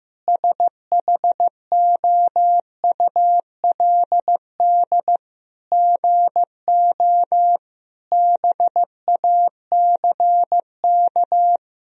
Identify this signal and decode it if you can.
Morse code, heard as a keyed tone, about 16 wpm, 700 Hz. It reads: SHOULD GO BACK
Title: Sentences from Top 1000 Words - Review Part 4 - 15wpm